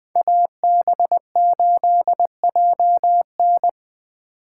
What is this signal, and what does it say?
Morse code, heard as a keyed tone, about 23 wpm, 700 Hz. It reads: AB8JN
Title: Speed Racing - USA 2x2 Call Signs 20wpm